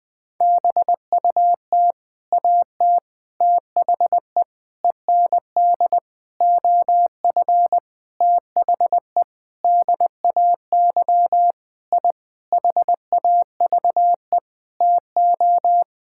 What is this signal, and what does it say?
Morse code, heard as a keyed tone, about 20 wpm, 700 Hz. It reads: BUT AT THE END OF THE DAY I HAVE TO
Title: Sentences with 8-Word Phrases - 20wpm